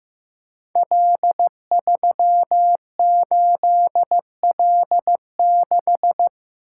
Morse code, heard as a keyed tone, about 16 wpm, 700 Hz. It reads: L38L6
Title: Code Groups - 5 Letter:Numbers - 15wpm